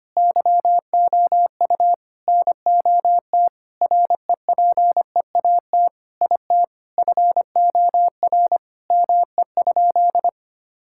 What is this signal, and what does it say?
Morse code, heard as a keyed tone, about 25 wpm, 700 Hz. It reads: YOU NOT REPEAT IT FOR ME?
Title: Sentences from Top 1600 Words 25wpm